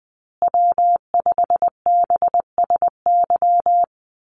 Morse code, heard as a keyed tone, about 20 wpm, 700 Hz. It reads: W5BSY
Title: USA Call Signs Repeated 2x 20wpm